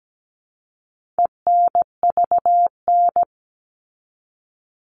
Morse code, heard as a keyed tone, about 17 wpm, 700 Hz. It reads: ENVN
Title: All Cut Numbers - 4-Digit 17wpm